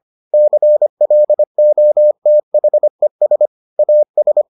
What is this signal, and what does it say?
Morse code, heard as a keyed tone, about 25 wpm, 600 Hz. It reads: CLOTHES AS